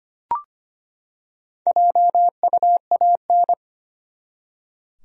Morse code, heard as a keyed tone, about 25 wpm, 700 Hz. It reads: JUAN
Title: USA Common Names - Max 5 Letters - 25wpm